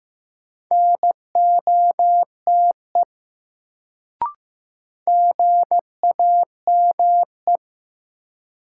Morse code, heard as a keyed tone, about 15 wpm, 700 Hz. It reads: NOTE GAME
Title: Sentences from Top 500 Words - New Word Drill - 15wpm